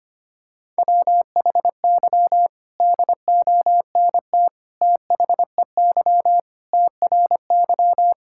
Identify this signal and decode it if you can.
Morse code, heard as a keyed tone, about 25 wpm, 700 Hz. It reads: WHY DONT THEY TRY
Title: Sentences from Top 1000 Words - Review Part 5 - 25wpm